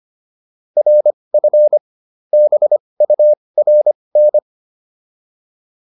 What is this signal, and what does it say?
Morse code, heard as a keyed tone, about 25 wpm, spaced as for 20 wpm, 600 Hz. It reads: RF BURN
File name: PR302 20